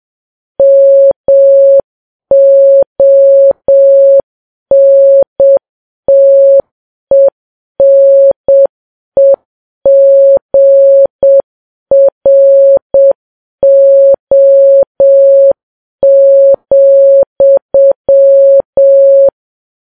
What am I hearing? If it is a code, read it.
Morse code, heard as a keyed tone, about 7 wpm, 550 Hz. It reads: MONTENEGRO,
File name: Files Web tls / Morse on9fv / morse7kps